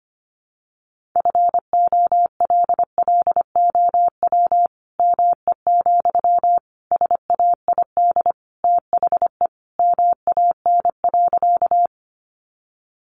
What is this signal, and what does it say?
Morse code, heard as a keyed tone, about 25 wpm, 700 Hz. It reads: FOLLOW ME, SAID THE MAN.
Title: Sentences from Top 200 Words 25wpm